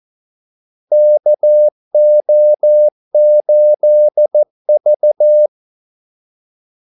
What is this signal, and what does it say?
Morse code, heard as a keyed tone, about 14 wpm, 600 Hz. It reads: KO8V
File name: Callsigns-14wpm